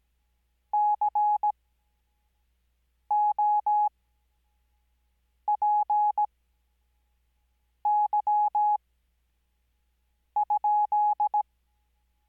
Morse code, heard as a keyed tone, about 17 wpm, spaced as for 5 wpm, 840 Hz. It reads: COPY?
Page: Farnsworth Spacing: 15WPM Character speed Spaced at 5WPM.